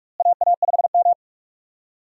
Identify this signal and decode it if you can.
Morse code, heard as a keyed tone, about 43 wpm, 700 Hz. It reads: AA5M
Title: USA Call Signs - 4 Characters - Repeated 2x - 45wpm